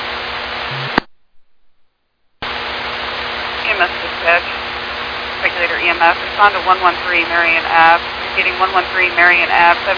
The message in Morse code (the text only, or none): none